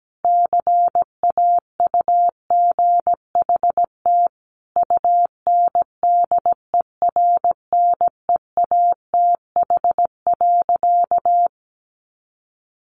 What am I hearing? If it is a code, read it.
Morse code, heard as a keyed tone, about 17 wpm, 700 Hz. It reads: CAUGHT UNDERNEATH.
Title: A Sound Of Thunder 17wpm